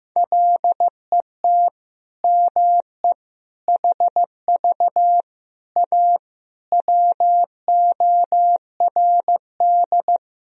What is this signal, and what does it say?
Morse code, heard as a keyed tone, about 15 wpm, 700 Hz. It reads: LET ME HV A WORD
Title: Sentences from Top 300 Words and Abbreviations 15wpm